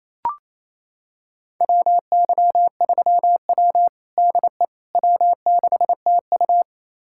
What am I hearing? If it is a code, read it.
Morse code, heard as a keyed tone, about 28 wpm, 700 Hz. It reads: WY3W DE W6TU
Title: USA Callsign DE Callsign - 4 Characters - 28wpm